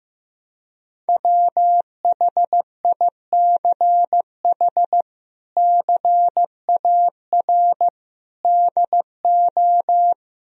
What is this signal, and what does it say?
Morse code, heard as a keyed tone, about 16 wpm, 700 Hz. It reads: WHICH CAR DO
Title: Sentences from Top 1000 Words - Review Part 1 - 15wpm